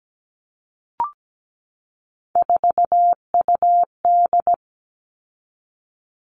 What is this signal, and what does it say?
Morse code, heard as a keyed tone, about 17 wpm, 700 Hz. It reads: 4UD
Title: All Cut Numbers - 3-Digit 17wpm